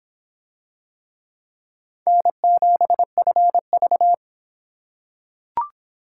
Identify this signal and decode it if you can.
Morse code, heard as a keyed tone, about 22 wpm, 700 Hz. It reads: N7FV